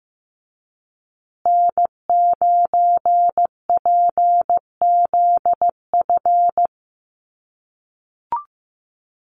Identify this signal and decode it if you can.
Morse code, heard as a keyed tone, about 15 wpm, 700 Hz. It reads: N9PZF
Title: USA Call Signs Repeated 3x 15wpm